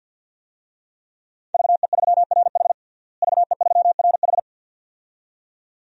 Morse code, heard as a keyed tone, about 52 wpm, 700 Hz. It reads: VE3RH VE3RH
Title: DXCC-100 Call Signs - 5 Characters - Repeated 2x - 50wpm